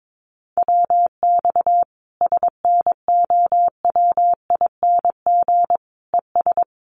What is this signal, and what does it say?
Morse code, heard as a keyed tone, about 22 wpm, 700 Hz. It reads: WX SNOWING ES
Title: Common QSO Phrases 22wpm